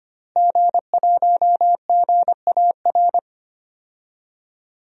Morse code, heard as a keyed tone, about 25 wpm, 700 Hz. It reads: G1GAR
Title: DXCC-100 Call Signs - 5 Characters - Repeated 2x - 25wpm